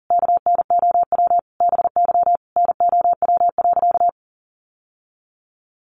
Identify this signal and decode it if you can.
Morse code, heard as a keyed tone, about 40 wpm, 700 Hz. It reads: KNOW BY NOW.
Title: Sentences from Top 1400 Words 40wpm